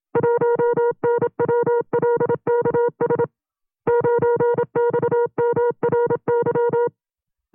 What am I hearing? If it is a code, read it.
Morse code, heard as a keyed tone, about 27 wpm, 480 Hz. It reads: 1NWLKS 9XMRY
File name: CW-practice/25WPM